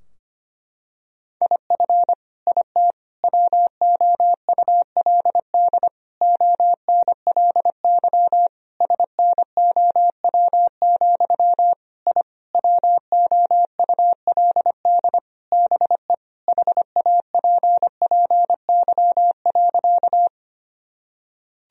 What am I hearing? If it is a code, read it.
Morse code, heard as a keyed tone, about 25 wpm, 700 Hz. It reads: IF IT WOULD ONLY SNOW, I WOULD BE HAPPY.
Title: Sentences from Top 1000 Words - Review Part 3 - 25wpm